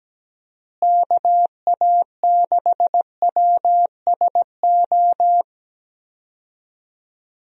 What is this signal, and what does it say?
Morse code, heard as a keyed tone, about 17 wpm, 700 Hz. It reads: KA6WSO
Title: USA Call Signs - 6 Characters - Repeated 2x - 17wpm